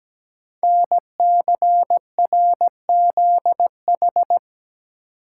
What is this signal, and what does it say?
Morse code, heard as a keyed tone, about 17 wpm, 700 Hz. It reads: NCRZH
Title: Code Groups - 5 Letters - 17wpm